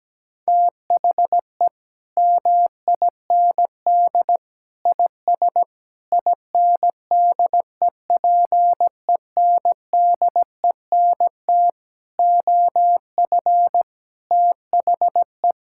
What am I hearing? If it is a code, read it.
Morse code, heard as a keyed tone, about 17 wpm, 700 Hz. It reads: THE MIND IS INDEPENDENT OF THE
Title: Sentences from Top 1000 Words - Review Part 6 - 17wpm